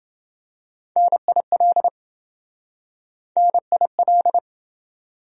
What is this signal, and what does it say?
Morse code, heard as a keyed tone, about 27 wpm, 700 Hz. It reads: NIL NIL